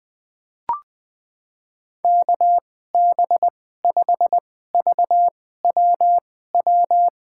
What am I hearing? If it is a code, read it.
Morse code, heard as a keyed tone, about 20 wpm, spaced as for 14 wpm, 700 Hz. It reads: KB5VWW
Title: USA Call Signs - 6 Characters - 2x Character Spacing - 20wpm